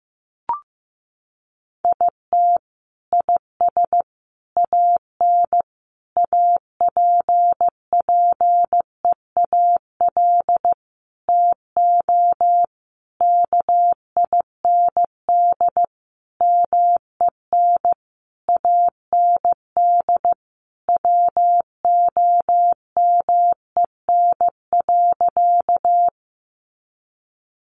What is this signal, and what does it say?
Morse code, heard as a keyed tone, about 15 wpm, 700 Hz. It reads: IT IS AN APPEAL TO KIND MEN AND WOMEN.
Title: Sentences from Top 1300 Words 15wpm